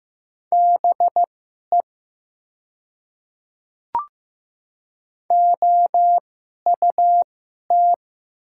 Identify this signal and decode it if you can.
Morse code, heard as a keyed tone, about 15 wpm, spaced as for 11 wpm, 700 Hz. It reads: BE OUT
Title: 2x Character Spacing - Top 100 Words - 15wpm